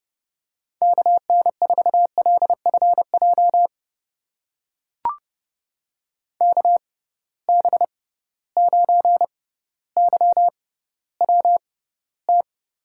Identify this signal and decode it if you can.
Morse code, heard as a keyed tone, about 30 wpm, spaced as for 15 wpm, 700 Hz. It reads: KN4LFJ KB9YWT
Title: USA Call Signs - 6 Characters - 6x Character Spacing - 30wpm